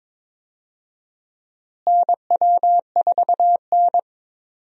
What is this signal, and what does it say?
Morse code, heard as a keyed tone, about 21 wpm, 700 Hz. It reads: NW4N
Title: USA Call Signs - 4 Characters - Repeated 2x - 22wpm